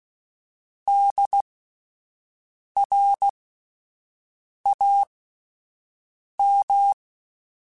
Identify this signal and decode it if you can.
Morse code, heard as a keyed tone, about 16 wpm, spaced as for 5 wpm, 800 Hz. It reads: DRAM